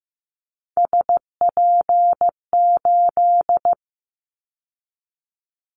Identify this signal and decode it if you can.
Morse code, heard as a keyed tone, about 15 wpm, 700 Hz. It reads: SP8
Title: Callsign Prefixes for Top 25 DXCC - 15wpm